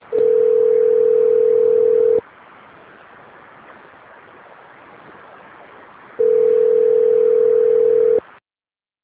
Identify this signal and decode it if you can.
Ringback tone (North America), 440 + 480 Hz.